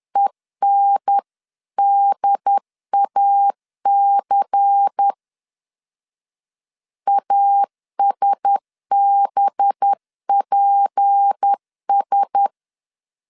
Morse code, heard as a keyed tone, about 10 wpm, 800 Hz. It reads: ENDAC ASBPS